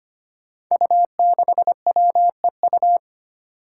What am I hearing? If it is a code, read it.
Morse code, heard as a keyed tone, about 25 wpm, 700 Hz. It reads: U6WEU
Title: Five Challenging Letter-Number Combinations 25wpm